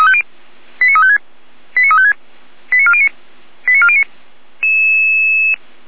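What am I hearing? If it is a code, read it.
Five-tone selective call (ZVEI), heard as quick runs of five tones, tones of about 80 ms. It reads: 89356 89356 89409 89409
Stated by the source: ZVEI five-tone selective call received by SDR.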